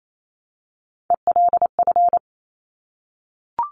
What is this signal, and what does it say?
Morse code, heard as a keyed tone, about 28 wpm, 700 Hz. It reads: ELF
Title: Top Three Letter Combinations in Words 28wpm